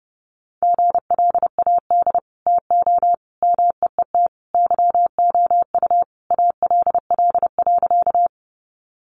Morse code, heard as a keyed tone, about 30 wpm, 700 Hz. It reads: GLAD TO MEET YOU ALL.
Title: Sentences from Top 1100 Words 30wpm